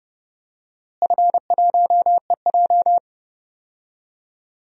Morse code, heard as a keyed tone, about 30 wpm, 700 Hz. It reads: F1EJ
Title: DXCC-100 Call Signs - 4 Characters - Repeated 2x - 30wpm